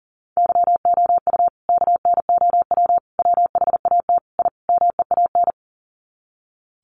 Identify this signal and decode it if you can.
Morse code, heard as a keyed tone, about 40 wpm, 700 Hz. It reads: YOU KNOW WHAT I MEAN
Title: Sets of 5 Words 40wpm